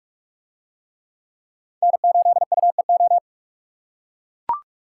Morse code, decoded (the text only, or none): N8UEO